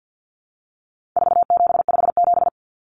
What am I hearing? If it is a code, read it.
Morse code, heard as a keyed tone, about 50 wpm, 700 Hz. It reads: 4757
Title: Numbers - 4-Digit 50wpm